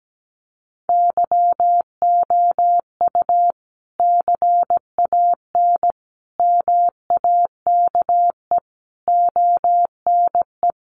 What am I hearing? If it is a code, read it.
Morse code, heard as a keyed tone, about 17 wpm, 700 Hz. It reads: YOU CAN MAKE ONE
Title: Sentences from Top 1000 Words 17wpm